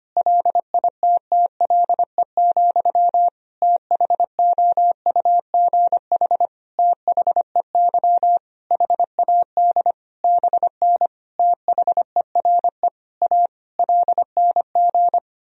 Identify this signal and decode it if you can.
Morse code, heard as a keyed tone, about 25 wpm, 700 Hz. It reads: LITTLE, THOUGH THEY HAD BN THERE A LNG
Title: Sentences from Top 400 Words and Abbreviations 25wpm